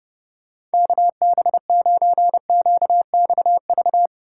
Morse code, heard as a keyed tone, about 31 wpm, 700 Hz. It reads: KB9QXV